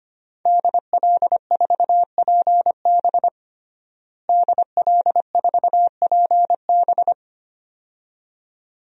Morse code, heard as a keyed tone, about 25 wpm, 700 Hz. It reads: DL4PB DL4PB
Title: DXCC-100 Call Signs - 5 Characters - Repeated 2x - 25wpm